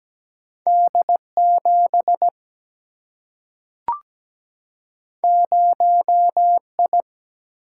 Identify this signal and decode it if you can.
Morse code, heard as a keyed tone, about 17 wpm, 700 Hz. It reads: D7 0I
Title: Single-Letter Single-Number 17wpm